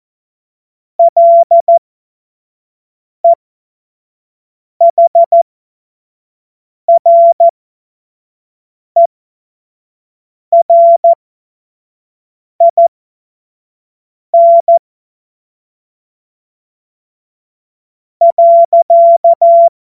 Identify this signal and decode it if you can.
Morse code, heard as a keyed tone, about 14 wpm, spaced as for 4 wpm, 680 Hz. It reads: LEHRERIN.
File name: Morsekurs / Witze / Witz 27. September 2022 25 F70